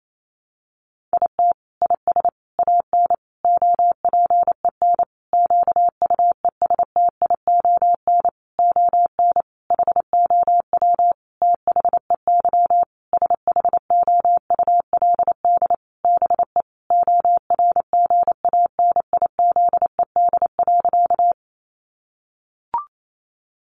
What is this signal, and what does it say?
Morse code, heard as a keyed tone, about 28 wpm, 700 Hz. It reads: IT IS AN OPEN QUESTION ON HOW THEY SHOULD BE ORGANIZED.
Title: Sentences from Top 1300 Words 28wpm